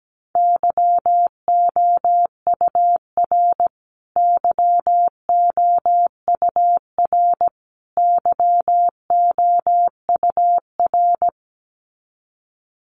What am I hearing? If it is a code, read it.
Morse code, heard as a keyed tone, about 17 wpm, 700 Hz. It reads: YOUR YOUR YOUR
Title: Top 500 Words Repeated 17wpm